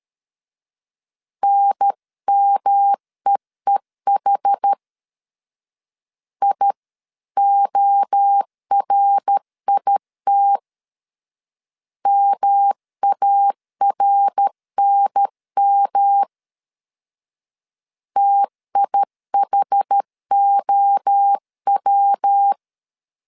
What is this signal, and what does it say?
Morse code, heard as a keyed tone, about 13 wpm, 800 Hz. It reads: NMEEH IORIT MARNM TIHOW
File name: CW / DET / 1 3 1500C S A 12wpm 2905